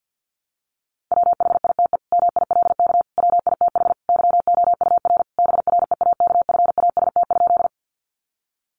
Morse code, heard as a keyed tone, about 50 wpm, 700 Hz. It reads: WHITE MILK WITH YOUR BREAKFAST?
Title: Sentences from Top 1500 Words 50wpm